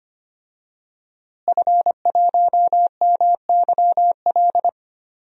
Morse code, heard as a keyed tone, about 25 wpm, 700 Hz. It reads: F1MYL
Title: DXCC-100 Call Signs - 5 Characters - Repeated 3x - 25wpm